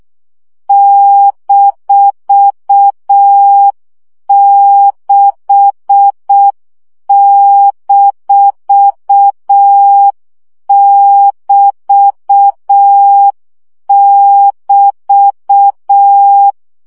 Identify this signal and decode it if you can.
Morse code, heard as a keyed tone, about 6 wpm, 800 Hz. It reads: -6-<BT><BT>
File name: CW gyakorlas b 6 egyenlo minusz 5 baud